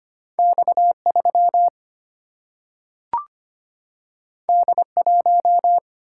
Morse code, heard as a keyed tone, about 25 wpm, 700 Hz. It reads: X3 D1
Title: Single-Letter Single-Number 25wpm